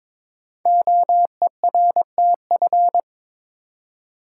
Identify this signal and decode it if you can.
Morse code, heard as a keyed tone, about 22 wpm, 700 Hz. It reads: OERTF
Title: Code Groups - 5 Letters - 22wpm